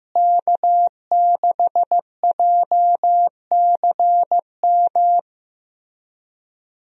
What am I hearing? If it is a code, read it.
Morse code, heard as a keyed tone, about 15 wpm, 700 Hz. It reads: K6JCM
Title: USA Call Signs - 5 Characters - Repeated 2x - 15wpm